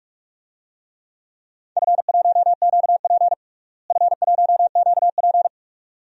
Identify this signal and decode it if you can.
Morse code, heard as a keyed tone, about 46 wpm, 700 Hz. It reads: F1QP F1QP
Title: DXCC-100 Call Signs - 4 Characters - Repeated 2x - 45wpm